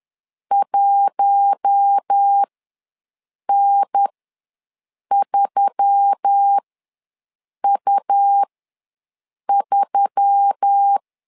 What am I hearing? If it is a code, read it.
Morse code, heard as a keyed tone, about 10 wpm, spaced as for 6 wpm, 800 Hz. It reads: 1N3U3
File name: LEZIONI / 4 Q1S3 ws 10wpm 2437